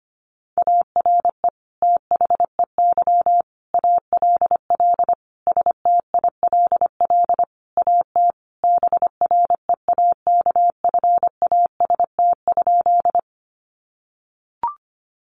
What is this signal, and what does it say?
Morse code, heard as a keyed tone, about 25 wpm, 700 Hz. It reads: ARE THEY ALL STILL AT BREAKFAST?
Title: Sentences from Top 1500 Words 25wpm